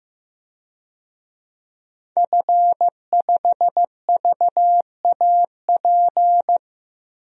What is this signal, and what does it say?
Morse code, heard as a keyed tone, about 15 wpm, 700 Hz. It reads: F5VAP
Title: DXCC-100 Call Signs - 5 Characters - 15wpm